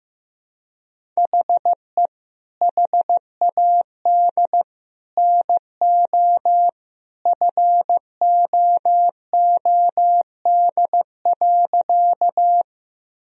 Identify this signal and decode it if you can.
Morse code, heard as a keyed tone, about 15 wpm, 700 Hz. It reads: HE HAD NO FOOD.
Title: Sentences from Top 1000 Words - Review Part 1 - 15wpm